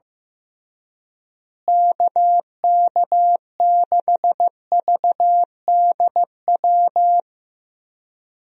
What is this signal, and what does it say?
Morse code, heard as a keyed tone, about 15 wpm, 700 Hz. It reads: KK6VDW
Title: USA Call Signs - 6 Characters - Repeated 3x - 15wpm